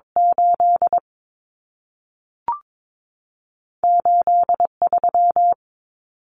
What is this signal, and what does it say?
Morse code, heard as a keyed tone, about 22 wpm, spaced as for 3 wpm, 700 Hz. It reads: 88 3
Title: Numbers - 2-Digit 22wpm